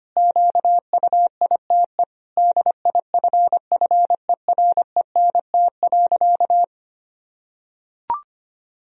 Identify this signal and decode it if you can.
Morse code, heard as a keyed tone, about 25 wpm, 700 Hz. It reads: QUITE DIFFERENT.
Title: Sentences from Top 1000 Words - Review Part 3 - 25wpm